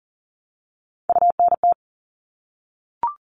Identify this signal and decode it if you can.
Morse code, heard as a keyed tone, about 40 wpm, 700 Hz. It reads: UNT
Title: All Cut Numbers - 3-Digit 40wpm